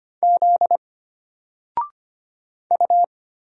Morse code, heard as a keyed tone, about 25 wpm, 700 Hz. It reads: Z U
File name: Single-Letter-Number-25wpm